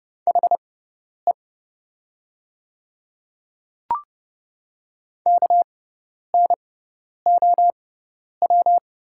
Morse code, heard as a keyed tone, about 30 wpm, spaced as for 10 wpm, 700 Hz. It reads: HE KNOW
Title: 6x Character Spacing - Top 100 Words - 30wpm